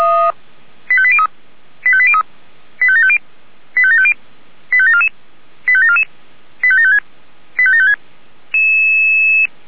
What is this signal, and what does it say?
Five-tone selective call (ZVEI), heard as quick runs of five tones, tones of about 70 ms. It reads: 85803 85803 85750 85750 85744 85744 85756 85756